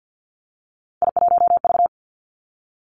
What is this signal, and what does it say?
Morse code, heard as a keyed tone, about 50 wpm, 700 Hz. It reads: I1V
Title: Three Challenging Letter-Number Combinations 50wpm